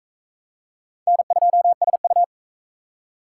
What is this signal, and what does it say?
Morse code, heard as a keyed tone, about 41 wpm, 700 Hz. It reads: N2SU